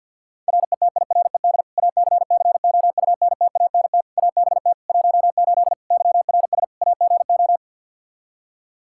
Morse code, heard as a keyed tone, about 51 wpm, 700 Hz. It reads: RETIRED ACCOUNTANT ABT 18 YRS AGO
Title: K1USN SST QSO Practice 50wpm